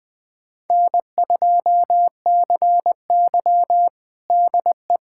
Morse code, heard as a keyed tone, about 21 wpm, 700 Hz. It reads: N2CY DE
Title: USA Callsign DE Callsign - 4 Characters - 20wpm